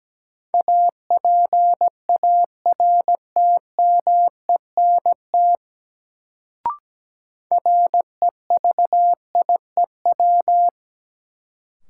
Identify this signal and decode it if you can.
Morse code, heard as a keyed tone, about 17 wpm, 700 Hz. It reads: APARTMENT REVIEW
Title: Sentences from Top 900 Words - New Word Drill - 17wpm